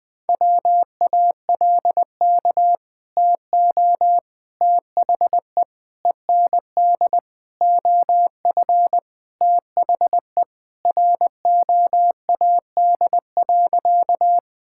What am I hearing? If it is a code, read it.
Morse code, heard as a keyed tone, about 21 wpm, 700 Hz. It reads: WALK TO THE END OF THE ROAD.
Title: Sentences with 4-Word Phrases - 20wpm